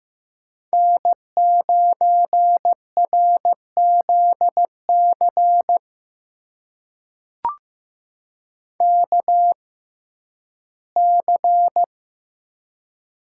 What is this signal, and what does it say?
Morse code, heard as a keyed tone, about 15 wpm, spaced as for 11 wpm, 700 Hz. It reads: N9RZC KC
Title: USA Call Signs - 5 Characters - 6x Character Spacing - 15wpm